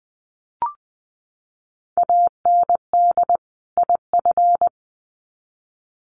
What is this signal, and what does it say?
Morse code, heard as a keyed tone, about 20 wpm, 700 Hz. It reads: AND IF
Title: Sets of 2 Words 20wpm